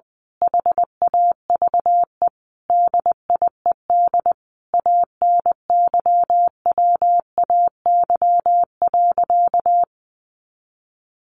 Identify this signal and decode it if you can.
Morse code, heard as a keyed tone, about 20 wpm, 700 Hz. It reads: HAVE DIED ANYWAY.
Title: A Sound Of Thunder 20wpm